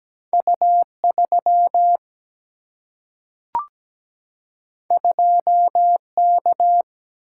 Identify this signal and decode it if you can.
Morse code, heard as a keyed tone, about 17 wpm, 700 Hz. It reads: U3 2K
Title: Single-Letter Single-Number 17wpm